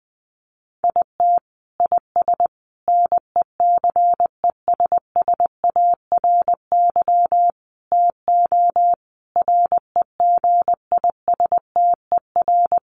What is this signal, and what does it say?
Morse code, heard as a keyed tone, about 20 wpm, 700 Hz. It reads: IT IS NECESSARY TO REGISTER
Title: Sentences from Top 1200 Words 20wpm